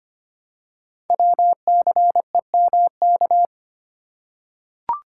Morse code, heard as a keyed tone, about 25 wpm, 700 Hz. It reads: WCEMK
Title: Code Groups - 5 Letters - 25wpm